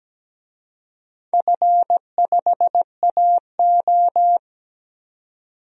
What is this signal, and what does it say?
Morse code, heard as a keyed tone, about 17 wpm, 700 Hz. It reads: F5AO
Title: DXCC-100 Call Signs - 4 Characters - 17wpm